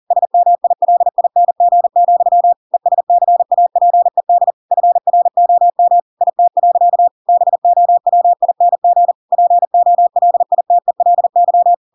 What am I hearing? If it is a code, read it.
Morse code, heard as a keyed tone, about 40 wpm, 700 Hz. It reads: SMILING, ESCAPED FROM IT. BOWING POLITELY
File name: Pinocchio / Pinocchio ch5 40 40 ik1hge